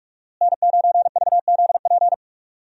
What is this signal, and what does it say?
Morse code, heard as a keyed tone, about 47 wpm, 700 Hz. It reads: N9VZP
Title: USA Call Signs - 5 Characters - Repeated 2x - 45wpm